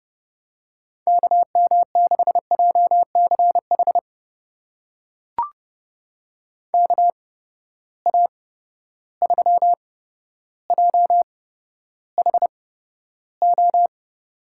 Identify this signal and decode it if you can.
Morse code, heard as a keyed tone, about 32 wpm, spaced as for 13 wpm, 700 Hz. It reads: KM6JCH KA3JHO
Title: USA Call Signs - 6 Characters - 8x Character Spacing - 30wpm